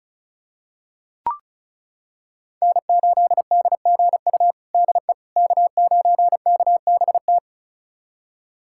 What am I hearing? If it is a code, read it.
Morse code, heard as a keyed tone, about 36 wpm, 700 Hz. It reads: N8DGU DE K9KBT
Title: USA Callsign DE Callsign - 5 Characters - 35wpm